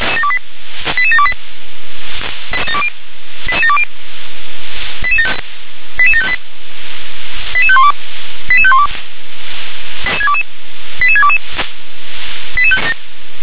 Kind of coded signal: five-tone selective call